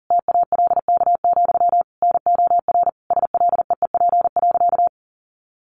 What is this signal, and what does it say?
Morse code, heard as a keyed tone, about 40 wpm, 700 Hz. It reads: TALK, NOR SLEEP.
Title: Sentences from Top 1500 Words 40wpm